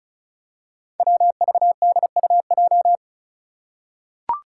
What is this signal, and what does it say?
Morse code, heard as a keyed tone, about 36 wpm, 700 Hz. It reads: WVDUJ
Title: Five Challenging Letter-Number Combinations 35wpm